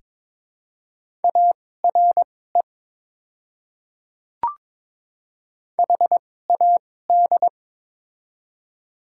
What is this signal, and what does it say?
Morse code, heard as a keyed tone, about 22 wpm, spaced as for 16 wpm, 700 Hz. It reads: ARE HAD
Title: 2x Character Spacing - Top 100 Words - 22wpm